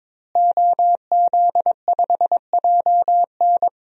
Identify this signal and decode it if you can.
Morse code, heard as a keyed tone, about 22 wpm, 700 Hz. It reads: OZ5JN
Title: DXCC-100 Call Signs - 5 Characters - Repeated 3x - 22wpm